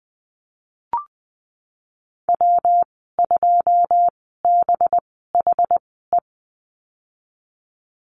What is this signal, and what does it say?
Morse code, heard as a keyed tone, about 19 wpm, spaced as for 14 wpm, 700 Hz. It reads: W2BHE
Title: 186 Lesson 30g - Letter b - Call Signs 20wpm